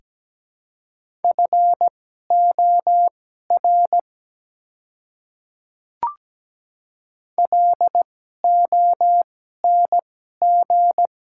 Morse code, heard as a keyed tone, about 17 wpm, spaced as for 12 wpm, 700 Hz. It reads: FOR LONG
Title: 2x Character Spacing - Top 300 Words - 17wpm